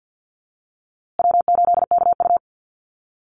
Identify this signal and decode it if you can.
Morse code, heard as a keyed tone, about 50 wpm, 700 Hz. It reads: W8KU
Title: USA Call Signs Repeated 2x 50wpm